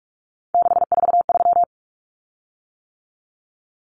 Morse code, heard as a keyed tone, about 44 wpm, 700 Hz. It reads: 643
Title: Common Cut Numbers - 3-Digit 45wpm